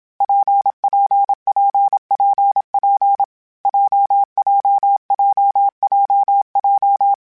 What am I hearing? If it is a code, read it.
Morse code, heard as a keyed tone, about 27 wpm, 800 Hz. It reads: PPPPP JJJJJ
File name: Variable Speed / Variable Speed-mp3s / WPJ WPJ1 sp @25